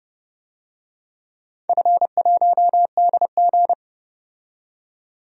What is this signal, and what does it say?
Morse code, heard as a keyed tone, about 30 wpm, 700 Hz. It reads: F1DG
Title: DXCC-100 Call Signs - 4 Characters - 30wpm